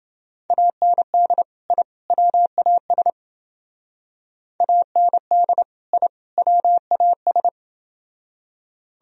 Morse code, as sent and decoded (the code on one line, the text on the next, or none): .- -. -.. / .. / .-- .- ... / .- -. -.. / .. / .-- .- ...
AND I WAS AND I WAS